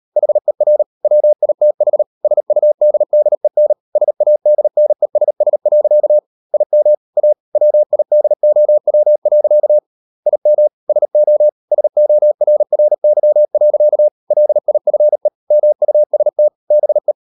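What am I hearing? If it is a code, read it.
Morse code, heard as a keyed tone, about 38 wpm, 600 Hz. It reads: HER WITH SUDDEN SADNESS. IM A WIDOW. IM SO SORRY. LIFE MUST BE
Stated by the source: Morse code practice at 38 WPM, sending Ken Follett's novel Never in English, from CW med Gnister.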